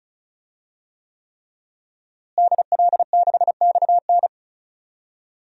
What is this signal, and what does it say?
Morse code, heard as a keyed tone, about 34 wpm, 700 Hz. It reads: DL6XN
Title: DXCC-100 Call Signs - 5 Characters - Repeated 3x - 35wpm